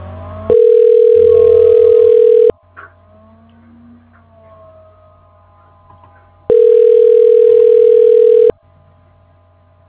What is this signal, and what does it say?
Ringback tone (North America), 440 + 480 Hz.
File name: recordings / RE6cd4d42b6647c39f1ca a